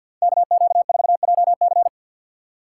Morse code, heard as a keyed tone, about 45 wpm, 700 Hz. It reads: KQ4JC
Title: USA Call Signs - 5 Characters - 2x Character Spacing - 50wpm